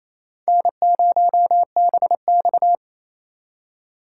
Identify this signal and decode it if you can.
Morse code, heard as a keyed tone, about 37 wpm, 700 Hz. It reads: N0BX